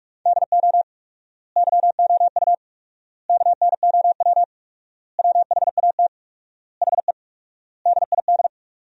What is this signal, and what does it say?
Morse code, heard as a keyed tone, about 47 wpm, 700 Hz. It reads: DO YOU KNOW WHAT HE DID
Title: 4x Word-Spacing - Sentences from Top 300 Words - 45wpm